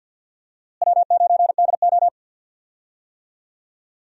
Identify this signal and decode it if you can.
Morse code, heard as a keyed tone, about 51 wpm, 700 Hz. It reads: W9DO
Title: USA Call Signs - 4 Characters - Repeated 3x - 50wpm